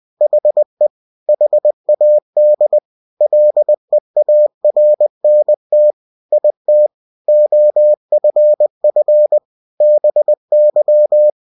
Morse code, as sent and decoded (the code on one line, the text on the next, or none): .... / .... .- -.. / .-.. .- .-. -. - / .. - / --- ..-. ..-. / -... -.--
HE HAD LEARNT IT OFF BY